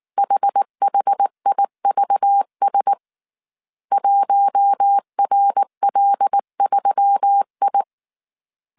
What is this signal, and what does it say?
Morse code, heard as a keyed tone, about 19 wpm, 800 Hz. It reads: HHIVS 1RL3I
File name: CW / DET / 1 6 1500C S A 18wpm 2255